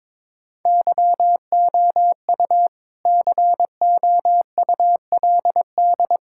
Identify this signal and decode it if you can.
Morse code, heard as a keyed tone, about 22 wpm, 700 Hz. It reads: YOU COULD
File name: practice / 22wpm / Sentences with 2-Word Phrases - 22wpm